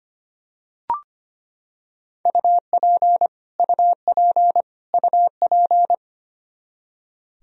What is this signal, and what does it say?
Morse code, heard as a keyed tone, about 25 wpm, 700 Hz. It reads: UP UP UP
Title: Top 100 Words Repeated 25wpm